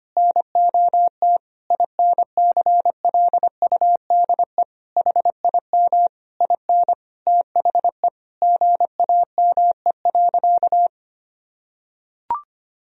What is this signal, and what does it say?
Morse code, heard as a keyed tone, about 25 wpm, 700 Hz. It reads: NOT INCLUDE HIM IN THE GAME.
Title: Sentences from Top 1000 Words - Review Part 3 - 25wpm